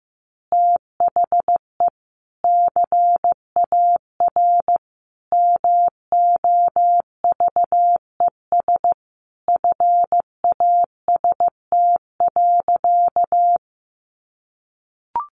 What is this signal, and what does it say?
Morse code, heard as a keyed tone, about 15 wpm, 700 Hz. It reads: THE CAR MOVES FAST.
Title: Easy Phrases 15wpm